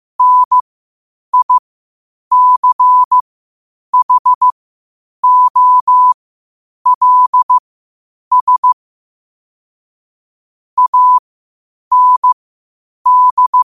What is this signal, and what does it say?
Morse code, heard as a keyed tone, about 15 wpm, spaced as for 8 wpm, 1000 Hz. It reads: NICHOLS AND
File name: qotdfiles / 8wpm